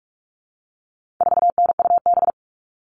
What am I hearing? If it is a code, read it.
Morse code, heard as a keyed tone, about 44 wpm, 700 Hz. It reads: 4NUB
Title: All Cut Numbers - 4-Digit 45wpm